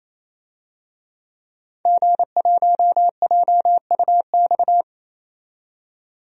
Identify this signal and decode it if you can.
Morse code, heard as a keyed tone, about 28 wpm, 700 Hz. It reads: G1JUX
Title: DXCC-100 Call Signs - 5 Characters - 28wpm